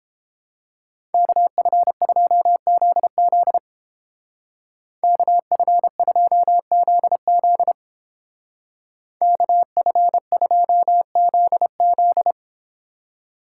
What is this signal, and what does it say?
Morse code, heard as a keyed tone, about 30 wpm, 700 Hz. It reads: KF2ZZ KF2ZZ KF2ZZ